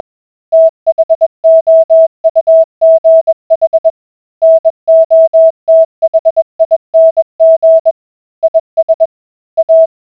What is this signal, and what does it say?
Morse code, heard as a keyed tone, about 20 wpm, 650 Hz. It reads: THOUGH NOTHING IS A